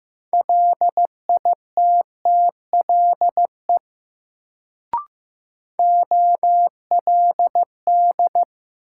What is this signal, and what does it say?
Morse code, heard as a keyed tone, about 15 wpm, 700 Hz. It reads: LITTLE OLD
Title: Sets of 2 Words - New Word Drill - 15wpm